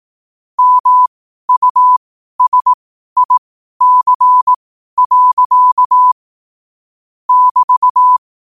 Morse code, heard as a keyed tone, about 18 wpm, spaced as for 12 wpm, 1000 Hz. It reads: MUSIC. <BT>
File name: qotdfiles / 12wpm